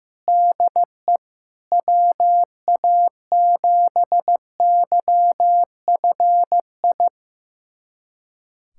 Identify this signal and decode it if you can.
Morse code, heard as a keyed tone, about 15 wpm, 700 Hz. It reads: DE WA7YFI
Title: USA Callsign DE Callsign - 6 Characters - 15wpm